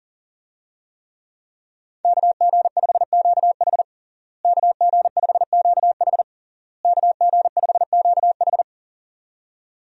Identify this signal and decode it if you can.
Morse code, heard as a keyed tone, about 40 wpm, 700 Hz. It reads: KG5QH KG5QH KG5QH
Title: USA Call Signs - 5 Characters - Repeated 3x - 40wpm